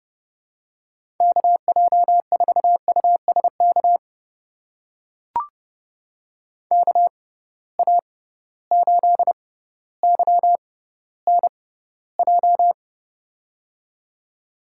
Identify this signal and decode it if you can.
Morse code, heard as a keyed tone, about 30 wpm, spaced as for 15 wpm, 700 Hz. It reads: KJ4USK KA8YNJ
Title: USA Call Signs - 6 Characters - 6x Character Spacing - 30wpm